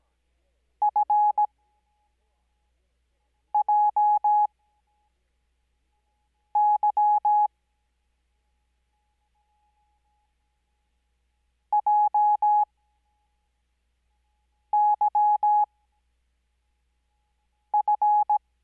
Morse code, heard as a keyed tone, about 17 wpm, spaced as for 4 wpm, 840 Hz. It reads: FJY JYF
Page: Farnsworth Spacing: 15WPM Character speed Spaced at 5WPM.